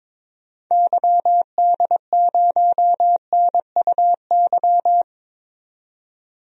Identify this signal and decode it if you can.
Morse code, heard as a keyed tone, about 22 wpm, 700 Hz. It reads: YD0NUY
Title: DXCC-100 Call Signs - 6 Characters - 22wpm